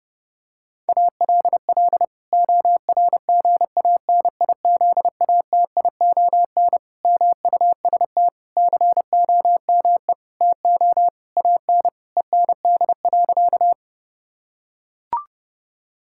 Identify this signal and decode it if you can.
Morse code, heard as a keyed tone, about 30 wpm, 700 Hz. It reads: ALL ORGANIZATION MUST COME TO AN END.
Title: Sentences from Top 1000 Words - Review Part 5 - 30wpm